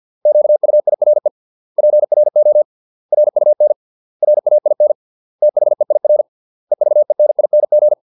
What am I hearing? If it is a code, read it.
Morse code, heard as a keyed tone, about 50 wpm, 600 Hz. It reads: QUIRE PRO RUN RAIN THEIR EVENING